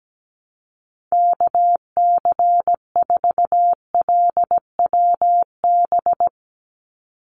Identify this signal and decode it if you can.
Morse code, heard as a keyed tone, about 17 wpm, 700 Hz. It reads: KC4LWB
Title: USA Call Signs 17wpm